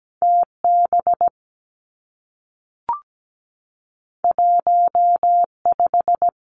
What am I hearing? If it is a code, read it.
Morse code, heard as a keyed tone, about 17 wpm, 700 Hz. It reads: T B 1 5
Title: Two Challenging Letter-Number Combinations 17wpm